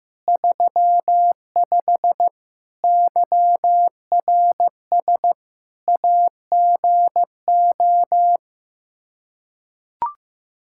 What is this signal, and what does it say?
Morse code, heard as a keyed tone, about 15 wpm, 700 Hz. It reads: 35 YRS AGO
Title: Practice QSO - Exchange 3 - More Info - 15wpm